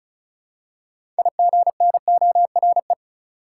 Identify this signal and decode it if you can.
Morse code, heard as a keyed tone, about 36 wpm, 700 Hz. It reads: IGNORE
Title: Six Letter Words - 35wpm